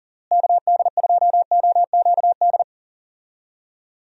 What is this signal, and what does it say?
Morse code, heard as a keyed tone, about 37 wpm, 700 Hz. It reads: KD2OQD
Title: USA Call Signs - 6 Characters - Repeated 3x - 40wpm